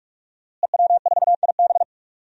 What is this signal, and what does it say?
Morse code, heard as a keyed tone, about 46 wpm, 700 Hz. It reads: EW4IB
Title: DXCC-100 Call Signs - 5 Characters - Repeated 3x - 45wpm